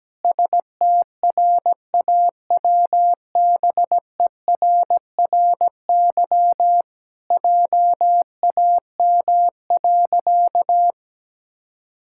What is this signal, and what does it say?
Morse code, heard as a keyed tone, about 17 wpm, 700 Hz. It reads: STRAWBERRY JAM.
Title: Pangram Challenge v2 - 17wpm